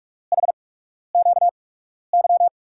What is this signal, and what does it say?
Morse code, heard as a keyed tone, about 44 wpm, spaced as for 12 wpm, 700 Hz. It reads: HQY
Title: USA Call Signs - 6 Characters - 8x Character Spacing - 45wpm